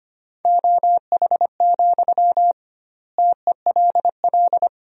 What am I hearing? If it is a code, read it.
Morse code, heard as a keyed tone, about 25 wpm, 700 Hz. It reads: OH, TELL
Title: 2x Word-Spacing - Sentences from Top 500 Words - 25wpm